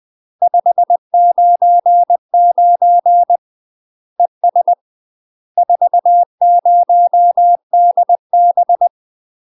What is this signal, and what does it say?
Morse code, heard as a keyed tone, about 20 wpm, 700 Hz. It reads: 599 ES 40DB
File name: UK MASTER QSOs TRAINING / 20WPM / UK MASTER QSOs 0030